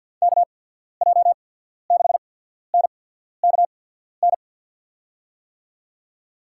Morse code, heard as a keyed tone, about 49 wpm, spaced as for 14 wpm, 690 Hz. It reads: KJ6NKN
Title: USA Call Signs - 6 Characters - 8x Character Spacing - 50wpm